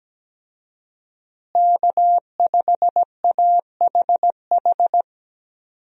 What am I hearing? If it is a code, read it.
Morse code, heard as a keyed tone, about 17 wpm, 700 Hz. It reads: K5AHH
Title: USA Call Signs - 5 Characters - Repeated 3x - 17wpm